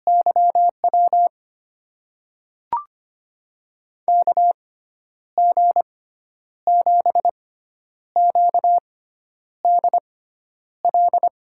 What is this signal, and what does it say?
Morse code, heard as a keyed tone, about 25 wpm, spaced as for 10 wpm, 700 Hz. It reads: YW KG7QDL
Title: USA Call Signs - 6 Characters - 6x Character Spacing - 25wpm